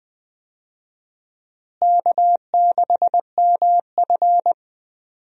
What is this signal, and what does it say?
Morse code, heard as a keyed tone, about 20 wpm, 700 Hz. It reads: K6MF
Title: USA Call Signs - 4 Characters - Repeated 3x - 20wpm